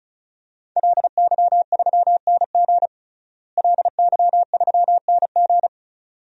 Morse code, heard as a keyed tone, about 36 wpm, 700 Hz. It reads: LY3NG LY3NG
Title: DXCC-100 Call Signs - 5 Characters - Repeated 2x - 35wpm